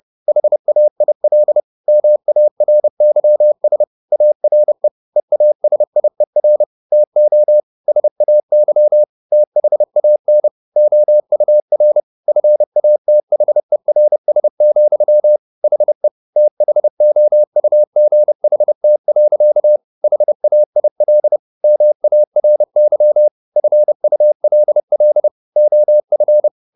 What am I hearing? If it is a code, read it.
Morse code, heard as a keyed tone, about 30 wpm, 600 Hz. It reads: HAIL MARYS ARE EASIER TO SAY THAN OUR FATHERS, HE THOUGHT. HAIL MARY FULL OF
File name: the old man and the sea / the old man and the sea 1 30wpm